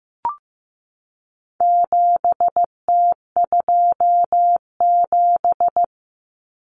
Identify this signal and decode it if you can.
Morse code, heard as a keyed tone, about 15 wpm, 700 Hz. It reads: 7T27
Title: Common Cut Numbers - 4-Digit 15wpm